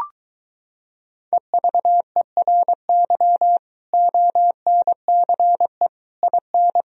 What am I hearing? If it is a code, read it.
Morse code, heard as a keyed tone, about 23 wpm, 700 Hz. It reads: EVERY ONCE IN